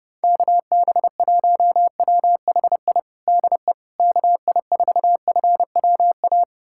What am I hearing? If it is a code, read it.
Morse code, heard as a keyed tone, about 30 wpm, 700 Hz. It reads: KB1WHI DE KI4FWA
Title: USA Callsign DE Callsign - 6 Characters - 30wpm